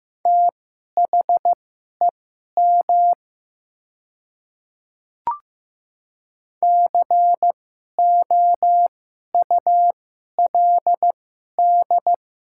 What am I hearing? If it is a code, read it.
Morse code, heard as a keyed tone, about 15 wpm, spaced as for 11 wpm, 700 Hz. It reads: THEM COULD
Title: 2x Character Spacing - Top 300 Words - 15wpm